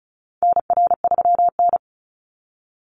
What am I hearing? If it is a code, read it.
Morse code, heard as a keyed tone, about 34 wpm, 700 Hz. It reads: NR3N
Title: Call Signs - 35wpm